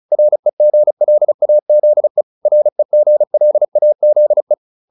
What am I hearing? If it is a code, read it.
Morse code, heard as a keyed tone, about 35 wpm, 600 Hz. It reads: REGLAZE REGLAZE